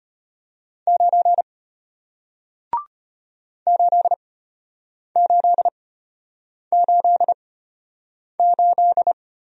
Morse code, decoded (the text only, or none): 9 8888